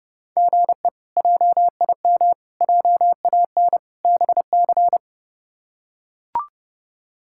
Morse code, heard as a keyed tone, about 30 wpm, 700 Hz. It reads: GE JIM JAN BC
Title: K1USN SST QSO Practice - 30wpm